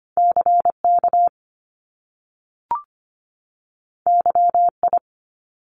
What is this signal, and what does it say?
Morse code, heard as a keyed tone, about 25 wpm, 700 Hz. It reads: CK YI
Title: Top Two Letter Combinations in Words 25wpm